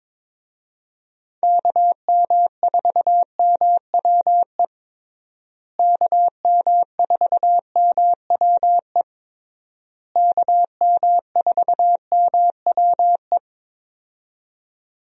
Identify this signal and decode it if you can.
Morse code, heard as a keyed tone, about 22 wpm, 700 Hz. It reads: KM4MWE KM4MWE KM4MWE
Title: USA Call Signs - 6 Characters - Repeated 3x - 22wpm